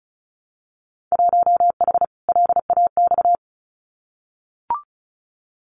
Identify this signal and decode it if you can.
Morse code, heard as a keyed tone, about 36 wpm, 700 Hz. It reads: 1H LAX
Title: Winter Field Day Exchanges 35wpm